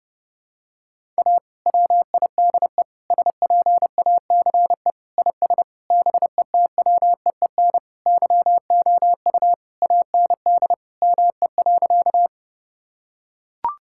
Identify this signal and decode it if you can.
Morse code, heard as a keyed tone, about 30 wpm, 700 Hz. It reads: A WIDE SPACE IS BETWEEN YOU AND ME.
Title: Sentences from Top 1000 Words - Review Part 6 - 30wpm